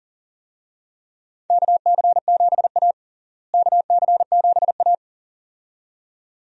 Morse code, heard as a keyed tone, about 40 wpm, 700 Hz. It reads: KC7A KC7A
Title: USA Call Signs - 4 Characters - Repeated 2x - 40wpm